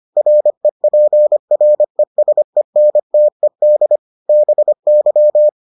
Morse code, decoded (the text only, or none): REPRESENTED BY